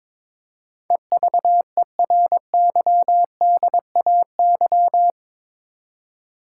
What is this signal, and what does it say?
Morse code, heard as a keyed tone, about 22 wpm, 700 Hz. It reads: EVERYDAY
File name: practice / 22wpm / Eight Letter Words - 22wpm